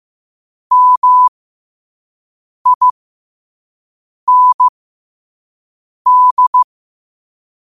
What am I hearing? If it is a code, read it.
Morse code, heard as a keyed tone, about 15 wpm, spaced as for 5 wpm, 1000 Hz. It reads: MIND
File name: qotdfiles / 5wpm